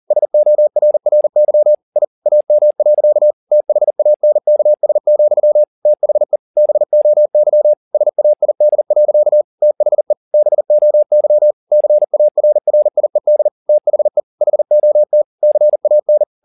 Morse code, heard as a keyed tone, about 40 wpm, 600 Hz. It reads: SORRY I AM. THANKS, THE BOY SAID. THE BOY CARRIED THE HOT CAN
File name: the old man and the sea 2 40wpm